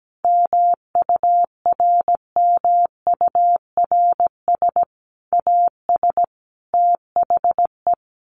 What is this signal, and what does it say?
Morse code, heard as a keyed tone, about 17 wpm, 700 Hz. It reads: MURMURS AS THE
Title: A Sound Of Thunder 17wpm